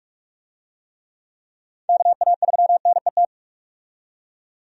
Morse code, heard as a keyed tone, about 46 wpm, 700 Hz. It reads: KA3NET
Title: USA Call Signs - 6 Characters - 45wpm